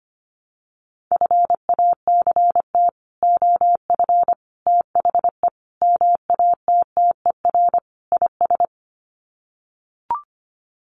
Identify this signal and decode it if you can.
Morse code, heard as a keyed tone, about 25 wpm, 700 Hz. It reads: FACT OF THE MATTER IS
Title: Sets of 5 Words 25wpm